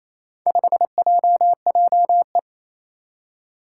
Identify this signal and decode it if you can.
Morse code, heard as a keyed tone, about 27 wpm, 700 Hz. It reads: 5JJE
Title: Four Challenging Letter-Number Combinations 28wpm